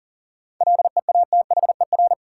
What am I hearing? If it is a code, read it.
Morse code, heard as a keyed tone, about 37 wpm, 700 Hz. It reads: LEATHER
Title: Seven Letter Words - 40wpm